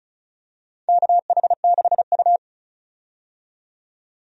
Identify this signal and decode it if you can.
Morse code, heard as a keyed tone, about 36 wpm, 700 Hz. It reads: KH6U
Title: DXCC-100 Call Signs - 4 Characters - 35wpm